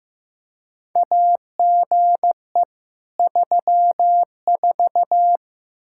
Morse code, heard as a keyed tone, about 15 wpm, 700 Hz. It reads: AGE 34
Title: Practice QSO - Exchange 3 - More Info - 15wpm